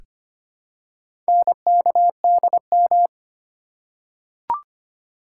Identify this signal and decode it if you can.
Morse code, heard as a keyed tone, about 25 wpm, 700 Hz. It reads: NKDM
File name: practice / 25wpm / Four Letter-Number Combinations - 25wpm